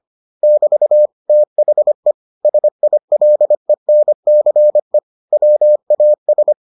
Morse code, heard as a keyed tone, about 25 wpm, 600 Hz. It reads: <BT> THE SILENCE WAS